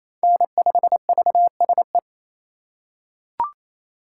Morse code, heard as a keyed tone, about 28 wpm, 700 Hz. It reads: N5VSE
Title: USA Call Signs - 5 Characters - 4x Character Spacing - 28wpm